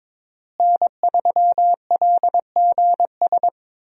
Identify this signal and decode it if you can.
Morse code, heard as a keyed tone, about 22 wpm, 700 Hz. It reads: N3LGS
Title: USA Call Signs - 5 Characters - Repeated 3x - 22wpm